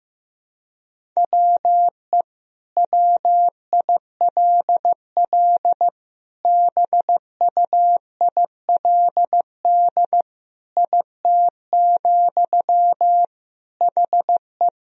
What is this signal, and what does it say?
Morse code, heard as a keyed tone, about 15 wpm, 700 Hz. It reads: WE WILL BUILD IT, HE
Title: Sentences from Top 1000 Words - Review Part 1 - 15wpm